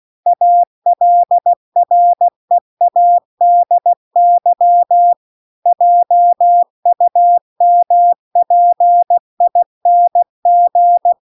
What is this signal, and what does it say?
Morse code, heard as a keyed tone, about 16 wpm, 700 Hz. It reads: ALREADY JUMPING